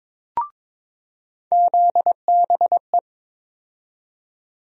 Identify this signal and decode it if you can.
Morse code, heard as a keyed tone, about 22 wpm, 700 Hz. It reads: ZBE
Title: USA Call Sign Postfixes - 3 Letters 22wpm